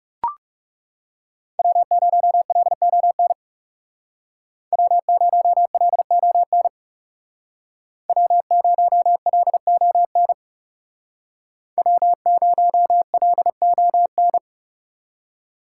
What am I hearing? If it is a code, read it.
Morse code, heard as a keyed tone, about 36 wpm, 700 Hz. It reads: W0LON W0LON W0LON W0LON
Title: Speed Racing - USA 1x3 Call Signs 30wpm